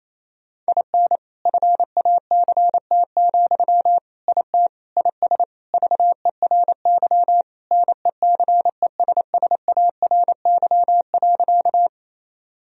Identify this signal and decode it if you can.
Morse code, heard as a keyed tone, about 27 wpm, 700 Hz. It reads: IN FACT, IT IS VERY NECESSARY.
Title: Sentences from Top 1000 Words - Review Part 3 - 28wpm